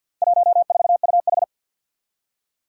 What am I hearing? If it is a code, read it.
Morse code, heard as a keyed tone, about 53 wpm, 700 Hz. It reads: 14UH
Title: Four Challenging Letter-Number Combinations 50wpm